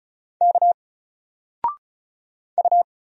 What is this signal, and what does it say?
Morse code, heard as a keyed tone, about 35 wpm, 700 Hz. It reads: K U
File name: Single-Letter-Number-35wpm